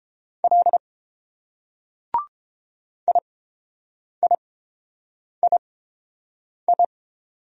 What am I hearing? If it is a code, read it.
Morse code, heard as a keyed tone, about 30 wpm, spaced as for 7 wpm, 700 Hz. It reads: L IIII